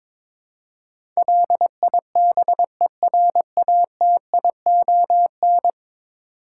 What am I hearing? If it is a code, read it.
Morse code, heard as a keyed tone, about 22 wpm, 700 Hz. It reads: LIBERATION
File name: practice / 22wpm / 10 Letter Words - First 3 Letters Given - 22wpm